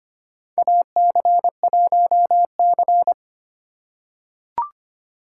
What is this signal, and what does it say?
Morse code, heard as a keyed tone, about 25 wpm, 700 Hz. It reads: AC1C
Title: USA Call Signs - 4 Characters - 2x Character Spacing - 25wpm